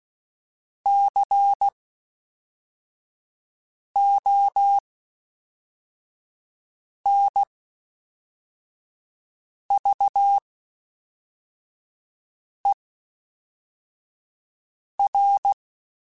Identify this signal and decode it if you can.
Morse code, heard as a keyed tone, about 16 wpm, 800 Hz. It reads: C O N V E R
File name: Special INGs / Special INGs-mp3s / Fast ing @10